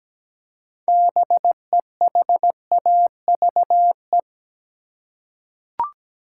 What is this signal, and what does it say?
Morse code, heard as a keyed tone, about 17 wpm, 700 Hz. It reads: BEHAVE
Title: Six Letter Words - 17wpm